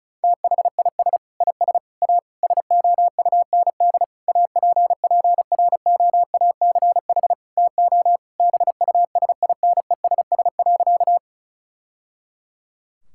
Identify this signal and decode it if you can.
Morse code, heard as a keyed tone, about 36 wpm, 700 Hz. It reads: THIS IS A SOUND APPROACH TO BUSINESS.
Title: Sentences from Top 1000 Words - Review Part 6 - 35wpm